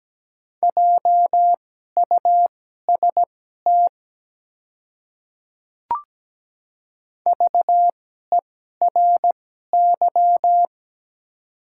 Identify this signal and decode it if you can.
Morse code, heard as a keyed tone, about 17 wpm, spaced as for 12 wpm, 700 Hz. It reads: JUST VERY
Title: 2x Character Spacing - Top 200 Words - 17wpm